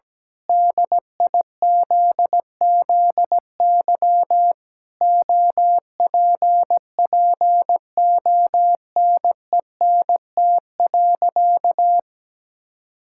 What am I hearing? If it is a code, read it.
Morse code, heard as a keyed tone, about 17 wpm, 700 Hz. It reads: DIZZY OPPONENT.
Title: Pangram Challenge v2 - 17wpm